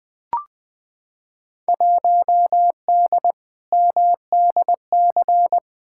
Morse code, heard as a keyed tone, about 21 wpm, 700 Hz. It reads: 1D MDC
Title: Field Day Exchanges - 20wpm